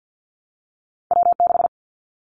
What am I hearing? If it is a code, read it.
Morse code, heard as a keyed tone, about 50 wpm, 700 Hz. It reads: W6
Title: Callsign Prefixes for Top 25 DXCC - Repeated 3x - 50wpm